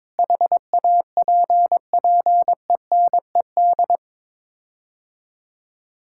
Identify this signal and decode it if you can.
Morse code, heard as a keyed tone, about 22 wpm, 700 Hz. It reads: HAPPENED
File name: practice / 22wpm / Sets of 8 Words - 22wpm